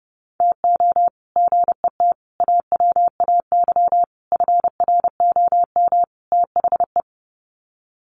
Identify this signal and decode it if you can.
Morse code, heard as a keyed tone, about 30 wpm, 700 Hz. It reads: TO GET AWAY FROM THE
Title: Sets of 5 Words Encore2 30wpm